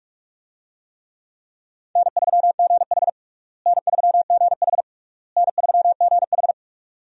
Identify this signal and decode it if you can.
Morse code, heard as a keyed tone, about 46 wpm, 700 Hz. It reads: N3GH N3GH N3GH
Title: USA Call Signs - 4 Characters - Repeated 3x - 45wpm